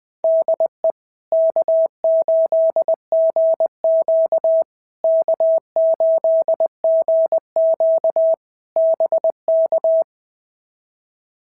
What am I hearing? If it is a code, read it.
Morse code, heard as a keyed tone, about 20 wpm, 650 Hz. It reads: DE K8GQ K8GQ BK
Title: POTA QSO Practice - 20wpm